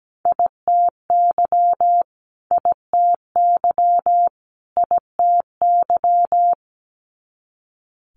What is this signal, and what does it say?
Morse code, heard as a keyed tone, about 17 wpm, 700 Hz. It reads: ITY ITY ITY
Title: Multi-letter Phonemes Repeated 3x 17wpm